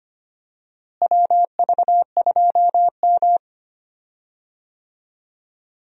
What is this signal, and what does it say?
Morse code, heard as a keyed tone, about 25 wpm, 700 Hz. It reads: WV2M
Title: USA Call Signs - 4 Characters - Repeated 2x - 25wpm